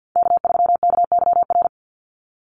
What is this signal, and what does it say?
Morse code, heard as a keyed tone, about 48 wpm, 700 Hz. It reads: K3KYR
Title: NA SOTA Chaser Callsigns 50wpm